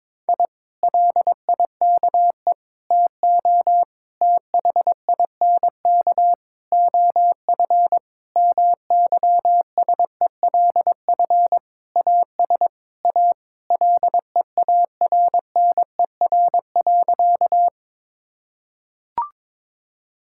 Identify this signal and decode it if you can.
Morse code, heard as a keyed tone, about 22 wpm, 700 Hz. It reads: I LIKE TO THINK OF MYSELF AS A LEARNER.
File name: practice / 22wpm / Sentences with 8-Word Phrases - 22wpm